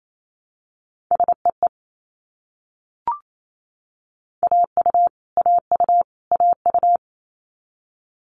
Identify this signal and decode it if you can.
Morse code, heard as a keyed tone, about 28 wpm, 700 Hz. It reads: SEE AU AU AU
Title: Multi-letter Phonemes Repeated 3x 28wpm